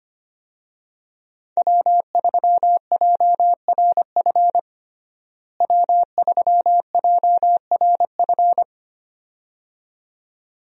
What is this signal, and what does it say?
Morse code, heard as a keyed tone, about 25 wpm, 700 Hz. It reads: W3JRF W3JRF
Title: USA Call Signs - 5 Characters - Repeated 2x - 25wpm